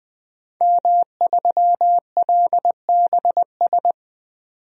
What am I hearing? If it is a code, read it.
Morse code, heard as a keyed tone, about 20 wpm, 700 Hz. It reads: M3LBS
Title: DXCC-100 Call Signs - 5 Characters - 20wpm